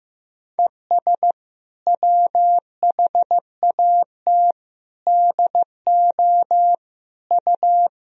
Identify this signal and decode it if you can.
Morse code, heard as a keyed tone, about 16 wpm, 700 Hz. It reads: ES WHAT DO U
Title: Sentences from Top 100 Words and Abbreviations 15wpm